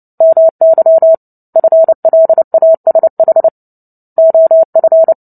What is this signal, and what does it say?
Morse code, heard as a keyed tone, about 30 wpm, 650 Hz. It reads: MY FLASH OF
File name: Book Two / 03-Chapter 3 - 29 WPM